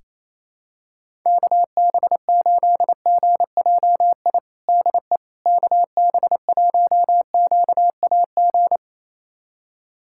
Morse code, heard as a keyed tone, about 28 wpm, 700 Hz. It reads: KB8GJI DE KB1QAG
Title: USA Callsign DE Callsign 28wpm